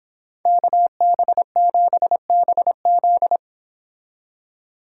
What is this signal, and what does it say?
Morse code, heard as a keyed tone, about 25 wpm, 700 Hz. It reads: KB7BZ
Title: Speed Racing - USA 2x2 Call Signs 22wpm